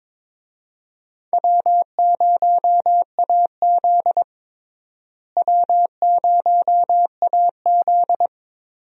Morse code, heard as a keyed tone, about 22 wpm, 700 Hz. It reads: W0AZ W0AZ
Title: USA Call Signs - 4 Characters - Repeated 3x - 22wpm